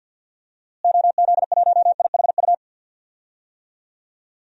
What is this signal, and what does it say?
Morse code, heard as a keyed tone, about 53 wpm, 700 Hz. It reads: OZ1IHU
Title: DXCC-100 Call Signs - 6 Characters - Repeated 2x - 50wpm